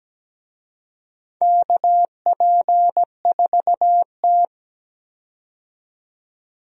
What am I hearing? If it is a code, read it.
Morse code, heard as a keyed tone, about 17 wpm, 700 Hz. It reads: KP4T
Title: DXCC-100 Call Signs - 4 Characters - 17wpm